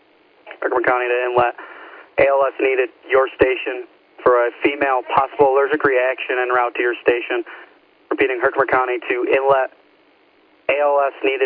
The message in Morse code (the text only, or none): none